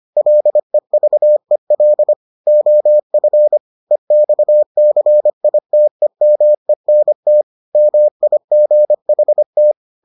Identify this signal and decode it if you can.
Morse code, heard as a keyed tone, about 25 wpm, 600 Hz. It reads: LEVEL OF EXCITEMENT MIGHT